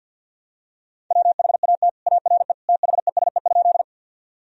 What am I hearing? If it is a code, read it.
Morse code, heard as a keyed tone, about 50 wpm, 700 Hz. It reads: WHAT ARE THESE?
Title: Sentences from Top 1000 Words - Review Part 1 - 50wpm